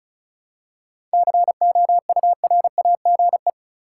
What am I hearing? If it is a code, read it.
Morse code, heard as a keyed tone, about 36 wpm, 700 Hz. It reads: COURAGE
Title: Seven Letter Words - 35wpm